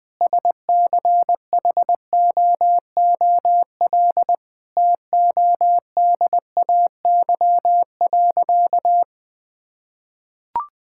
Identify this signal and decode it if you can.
Morse code, heard as a keyed tone, about 20 wpm, 700 Hz. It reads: SCHOOL TODAY.
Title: Sentences from Top 1000 Words - Review Part 5 - 20wpm